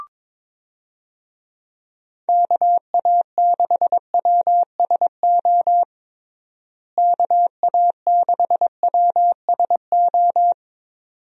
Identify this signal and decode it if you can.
Morse code, heard as a keyed tone, about 22 wpm, 700 Hz. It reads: KA6WSO KA6WSO
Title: USA Call Signs - 6 Characters - Repeated 2x - 22wpm